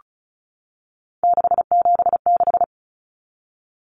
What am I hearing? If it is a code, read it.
Morse code, heard as a keyed tone, about 34 wpm, 700 Hz. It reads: 676
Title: Numbers - 3-Digit 35wpm